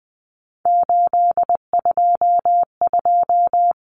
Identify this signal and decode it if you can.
Morse code, heard as a keyed tone, about 20 wpm, 700 Hz. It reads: 822
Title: Common Cut Numbers - 3-Digit 20wpm